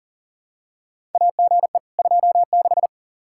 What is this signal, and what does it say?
Morse code, heard as a keyed tone, about 40 wpm, 700 Hz. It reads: AGE 26
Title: Practice QSO - Exchange 3 - More Info - 40wpm